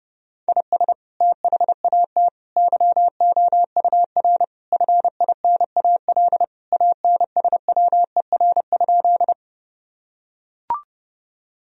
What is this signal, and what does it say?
Morse code, heard as a keyed tone, about 32 wpm, 700 Hz. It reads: IS THAT YOUR FINAL ANSWER?
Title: Sentences from Top 1000 Words - Review Part 2 - 30wpm